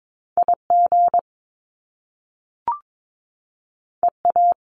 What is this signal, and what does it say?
Morse code, heard as a keyed tone, about 22 wpm, 700 Hz. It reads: IG EA
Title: Top Two Letter Combinations in Words 22wpm